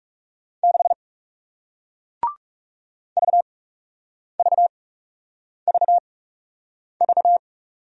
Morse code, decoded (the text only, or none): 6 VVVV